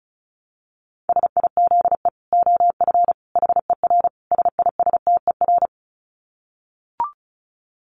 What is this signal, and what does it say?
Morse code, heard as a keyed tone, about 34 wpm, 700 Hz. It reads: SIZE OF HER SISTER
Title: Dit Challenge 35wpm